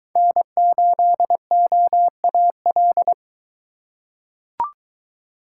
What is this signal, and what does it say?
Morse code, heard as a keyed tone, about 23 wpm, 700 Hz. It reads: N8OAL